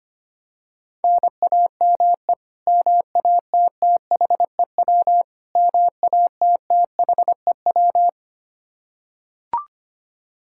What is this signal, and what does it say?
Morse code, heard as a keyed tone, about 25 wpm, 700 Hz. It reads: NAME MATTHEW MATTHEW
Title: K1USN SST QSO Practice 25wpm